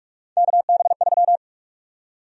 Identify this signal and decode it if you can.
Morse code, heard as a keyed tone, about 46 wpm, 700 Hz. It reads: KB3
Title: USA Call Sign Prefixes - 45wpm